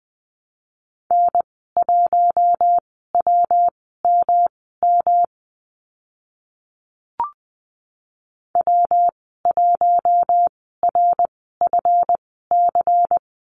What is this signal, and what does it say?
Morse code, heard as a keyed tone, about 20 wpm, spaced as for 14 wpm, 700 Hz. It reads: N1WMM W1RFC
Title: 167 Lesson 27g - Number 7 - Call Signs 20wpm